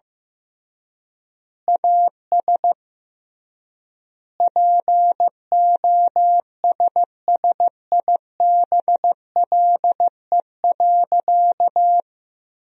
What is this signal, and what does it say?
Morse code, heard as a keyed tone, about 16 wpm, 700 Hz. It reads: AS POSSIBLE.
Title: 3x Word-Spacing - Sentences from Top 500 Words - 15wpm